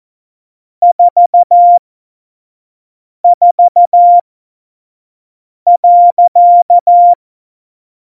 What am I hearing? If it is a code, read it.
Morse code, heard as a keyed tone, about 14 wpm, spaced as for 5 wpm, 700 Hz. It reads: 44.
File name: Morsekurs / Witze / Witz 18. Februar 2020, 25 F70